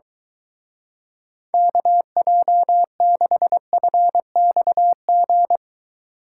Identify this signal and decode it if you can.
Morse code, heard as a keyed tone, about 23 wpm, 700 Hz. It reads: KJ6FXG